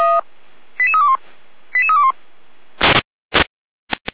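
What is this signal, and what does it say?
Five-tone selective call (ZVEI), heard as quick runs of five tones, tones of about 70 ms. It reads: 88321 88321
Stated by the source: ZVEI five-tone selective call received on SDR.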